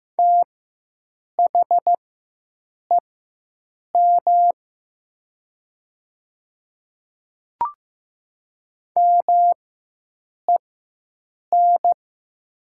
Morse code, heard as a keyed tone, about 15 wpm, spaced as for 7 wpm, 700 Hz. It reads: THEM MEN
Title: 4x Character Spacing - Top 200 Words - 15wpm